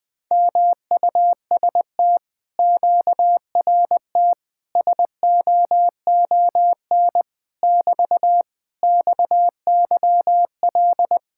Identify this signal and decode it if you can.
Morse code, heard as a keyed tone, about 20 wpm, 700 Hz. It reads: MUST QRT SOON <BT> XYL
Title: Speed-Racing Common QSO Phrases - 20wpm